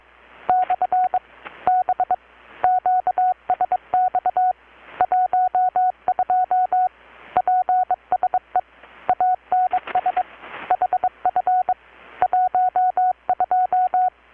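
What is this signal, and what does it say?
Morse code, heard as a keyed tone, about 22 wpm, 720 Hz. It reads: / B QSX 12 PSE ANS HF 12
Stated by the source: Morse from KPH's Night of Nights XX maritime radiotelegraphy event, received in Chiapas, Mexico about 3,800 km away on a Kenwood R-600 with a grounded 10 m long-wire antenna.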